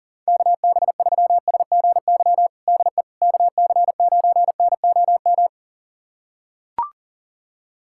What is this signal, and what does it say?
Morse code, heard as a keyed tone, about 40 wpm, 700 Hz. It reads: KB3SGY DE KC9NOM
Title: USA Callsign DE Callsign - 6 Characters - 40wpm